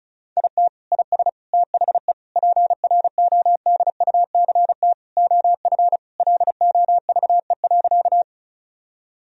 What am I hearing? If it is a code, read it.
Morse code, heard as a keyed tone, about 36 wpm, 700 Hz. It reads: IT IS THE PRODUCT OF LOVE.
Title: Sentences from Top 1000 Words - Review Part 2 - 35wpm